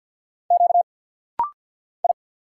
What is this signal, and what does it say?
Morse code, heard as a keyed tone, about 50 wpm, 690 Hz. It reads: Q I
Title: Single Letter-Number - Rapid-Fire 50wpm